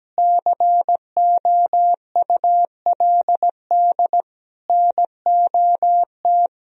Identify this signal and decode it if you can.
Morse code, heard as a keyed tone, about 17 wpm, 700 Hz. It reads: COULD NOT
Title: Sentences from Top 1000 Words - Review Part 2 - 17wpm